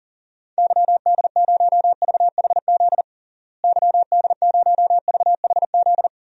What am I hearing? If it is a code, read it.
Morse code, heard as a keyed tone, about 40 wpm, 700 Hz. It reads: YD0VHZ YD0VHZ
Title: DXCC-100 Call Signs - 6 Characters - Repeated 2x - 40wpm